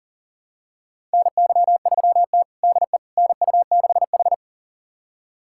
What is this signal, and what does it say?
Morse code, heard as a keyed tone, about 40 wpm, 700 Hz. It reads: NY3T DE NU6H
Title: USA Callsign DE Callsign - 4 Characters - 40wpm